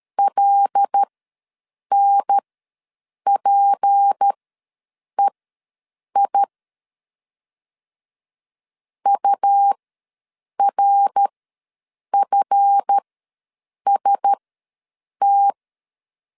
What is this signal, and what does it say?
Morse code, heard as a keyed tone, about 13 wpm, spaced as for 7 wpm, 800 Hz. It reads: LNPEI URFST
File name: LEZIONI / 7 GF9Z / 7 GF ws 12wpm 3140